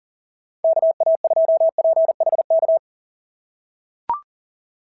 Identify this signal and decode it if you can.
Morse code, heard as a keyed tone, about 40 wpm, 690 Hz. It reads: KA2PHK
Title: SOTA QSO Practice 40wpm